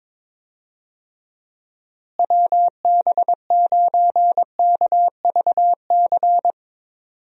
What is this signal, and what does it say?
Morse code, heard as a keyed tone, about 22 wpm, 700 Hz. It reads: WB9KVC
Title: USA Call Signs - 6 Characters - Repeated 3x - 22wpm